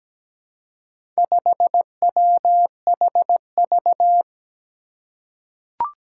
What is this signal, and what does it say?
Morse code, heard as a keyed tone, about 17 wpm, 700 Hz. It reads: 5WHV
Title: Four Challenging Letter-Number Combinations 17wpm